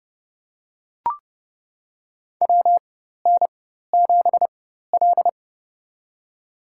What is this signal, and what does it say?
Morse code, heard as a keyed tone, about 30 wpm, spaced as for 14 wpm, 700 Hz. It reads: WN7L
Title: USA Call Signs - 4 Characters - 4x Character Spacing - 30wpm